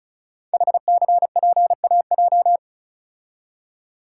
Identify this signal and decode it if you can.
Morse code, heard as a keyed tone, about 36 wpm, 700 Hz. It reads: HCPAJ
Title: Code Groups - 5 Letters - 35wpm